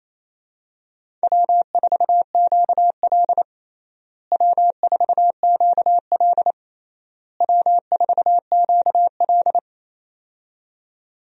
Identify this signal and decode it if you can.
Morse code, heard as a keyed tone, about 28 wpm, 700 Hz. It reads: W4QL W4QL W4QL
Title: USA Call Signs - 4 Characters - Repeated 3x - 28wpm